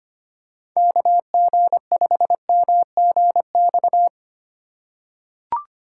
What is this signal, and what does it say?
Morse code, heard as a keyed tone, about 25 wpm, 700 Hz. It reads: KG5MGX
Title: USA Call Signs - 6 Characters - 2x Character Spacing - 25wpm